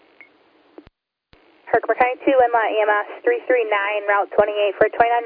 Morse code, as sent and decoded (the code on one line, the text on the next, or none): none